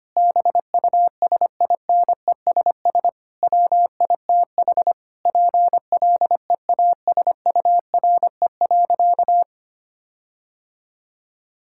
Morse code, heard as a keyed tone, about 25 wpm, 700 Hz. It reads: BUSINESS WITH PLEASURE.
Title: Sentences from Top 1600 Words 25wpm